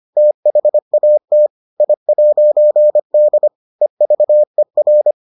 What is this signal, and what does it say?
Morse code, heard as a keyed tone, about 25 wpm, 600 Hz. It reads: THAT I'D EVER